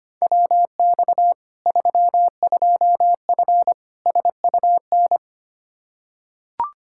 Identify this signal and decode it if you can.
Morse code, heard as a keyed tone, about 25 wpm, 700 Hz. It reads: WX 32F SUN
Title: Practice QSO - Exchange 2 - Info - 25wpm